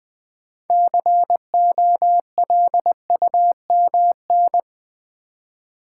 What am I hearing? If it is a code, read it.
Morse code, heard as a keyed tone, about 20 wpm, 700 Hz. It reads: COLUMN
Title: Build A Word - 3 Letters At A Time - 9-Letter Words - 20wpm